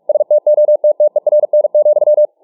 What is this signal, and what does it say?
Morse code, heard as a keyed tone, about 44 wpm, 600 Hz. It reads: STOTTERN,